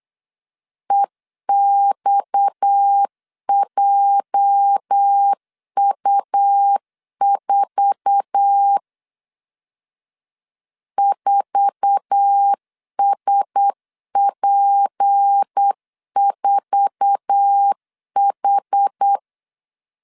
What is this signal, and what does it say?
Morse code, heard as a keyed tone, about 8 wpm, 800 Hz. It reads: EXJU4 4SP4H
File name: LEZIONI / 8 YJ45 / 8 YJ4 8wpm 3246